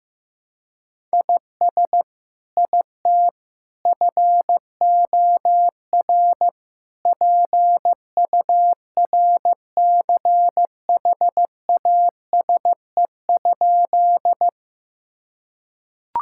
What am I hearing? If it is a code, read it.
Morse code, heard as a keyed tone, about 15 wpm, 700 Hz. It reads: IS IT FOR PURCHASE?
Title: Sentences from Top 1000 Words - Review Part 5 - 15wpm